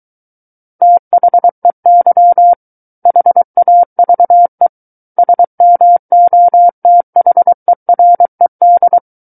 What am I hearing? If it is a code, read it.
Morse code, heard as a keyed tone, about 23 wpm, 700 Hz. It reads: THEY HAVE SMOTHERED
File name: Book One / 14-Chapter 14 - 23 WPM